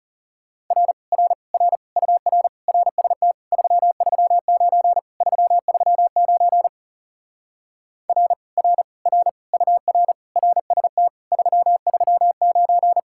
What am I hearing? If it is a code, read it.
Morse code, heard as a keyed tone, about 40 wpm, 700 Hz. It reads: R R R UR RST 339 339 R R R UR RST 339